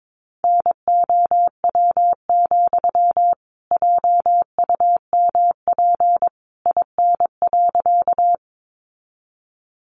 Morse code, heard as a keyed tone, about 22 wpm, 700 Hz. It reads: NOW, JUMP IN.
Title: Sentences from Top 1300 Words 22wpm